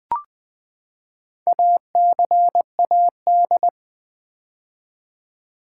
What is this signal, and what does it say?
Morse code, heard as a keyed tone, about 20 wpm, 700 Hz. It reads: ACAD
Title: Build A Word - 4 Letters At A Time - 8-Letter Words - 20wpm